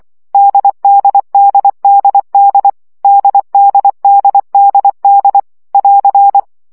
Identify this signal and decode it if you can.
Morse code, heard as a keyed tone, about 24 wpm, 800 Hz. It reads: DDDDD DDDDD <AR>
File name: CW gyakorlas d 20 baud